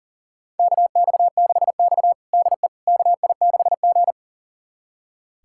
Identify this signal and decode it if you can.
Morse code, heard as a keyed tone, about 40 wpm, 700 Hz. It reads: KX6X DE KI6G
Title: USA Callsign DE Callsign - 4 Characters - 40wpm